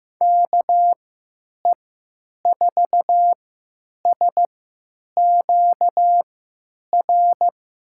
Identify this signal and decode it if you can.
Morse code, heard as a keyed tone, about 15 wpm, spaced as for 9 wpm, 700 Hz. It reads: KE4SQR
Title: USA Call Signs - 6 Characters - 3x Character Spacing - 15wpm